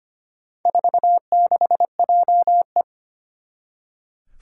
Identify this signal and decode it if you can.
Morse code, heard as a keyed tone, about 25 wpm, 700 Hz. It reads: 46JE
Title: Four Challenging Letter-Number Combinations 25wpm